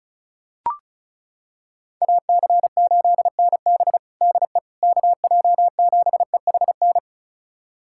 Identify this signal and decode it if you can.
Morse code, heard as a keyed tone, about 36 wpm, 700 Hz. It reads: AC8NB DE KJ7EHN
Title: USA Callsign DE Callsign 35wpm